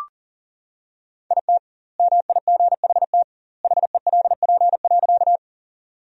Morse code, heard as a keyed tone, about 40 wpm, 700 Hz. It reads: IT MIGHT HELP.
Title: 2x Word-Spacing - Sentences from Top 300 Words - 40wpm